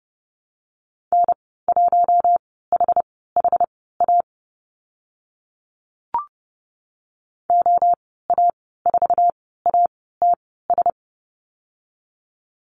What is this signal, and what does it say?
Morse code, heard as a keyed tone, about 31 wpm, spaced as for 17 wpm, 700 Hz. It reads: N1HHA OA4ATS
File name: Lesson 14g - Number 2 - Call Signs 30wpm